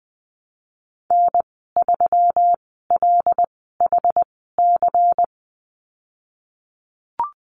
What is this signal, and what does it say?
Morse code, heard as a keyed tone, about 20 wpm, spaced as for 15 wpm, 700 Hz. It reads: N3LHC
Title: 186 Lesson 30g - Letter b - Call Signs 20wpm